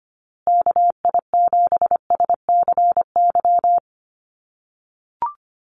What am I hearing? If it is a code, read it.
Morse code, heard as a keyed tone, about 25 wpm, 700 Hz. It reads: KI7SCY
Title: USA Call Signs Repeated 3x 25wpm